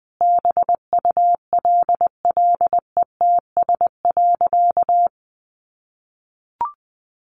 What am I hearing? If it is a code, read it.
Morse code, heard as a keyed tone, about 20 wpm, 700 Hz. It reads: BULLETS.
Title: A Sound Of Thunder 20wpm